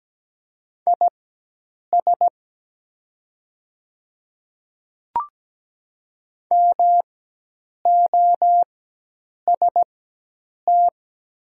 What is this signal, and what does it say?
Morse code, heard as a keyed tone, about 17 wpm, spaced as for 8 wpm, 700 Hz. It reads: IS MOST
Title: 4x Character Spacing - Top 100 Words - 17wpm